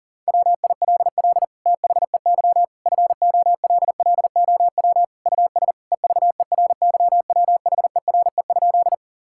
Morse code, heard as a keyed tone, about 40 wpm, 700 Hz. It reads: WILL THEY FOLLOW US EVERYWHERE?
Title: Sentences from Top 1600 Words 40wpm